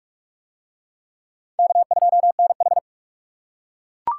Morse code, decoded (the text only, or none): K2NH